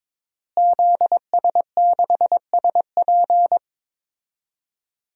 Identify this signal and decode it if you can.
Morse code, heard as a keyed tone, about 22 wpm, 700 Hz. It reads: ZS6SP
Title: DXCC-100 Call Signs - 5 Characters - Repeated 3x - 22wpm